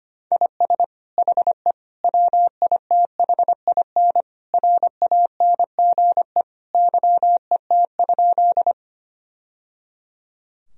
Morse code, heard as a keyed tone, about 25 wpm, 700 Hz. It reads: IS HE WITHIN RANGE YET?
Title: Sentences from Top 1000 Words - Review Part 4 - 25wpm